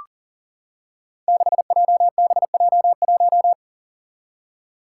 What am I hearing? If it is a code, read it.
Morse code, heard as a keyed tone, about 40 wpm, 700 Hz. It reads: 6JBJ1
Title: Five Challenging Letter-Number Combinations 40wpm